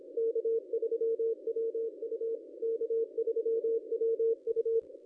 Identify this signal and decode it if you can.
Morse code, heard as a keyed tone, about 27 wpm, 410 Hz. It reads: K3WU K3WU
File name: contests / ss-cw-k5tr-op-n5rz -1100